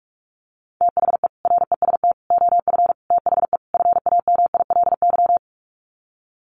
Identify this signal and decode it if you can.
Morse code, heard as a keyed tone, about 44 wpm, 700 Hz. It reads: THE REST OF THE FAMILY
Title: Sets of 5 Words Encore2 45wpm